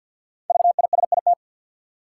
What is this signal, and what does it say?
Morse code, heard as a keyed tone, about 51 wpm, 700 Hz. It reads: VISIT
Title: Sentences from Top 900 Words - New Word Drill - 50wpm